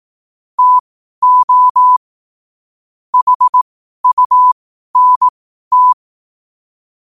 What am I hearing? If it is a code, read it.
Morse code, heard as a keyed tone, about 18 wpm, spaced as for 12 wpm, 1000 Hz. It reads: TO HUNT
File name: qotdfiles / 12wpm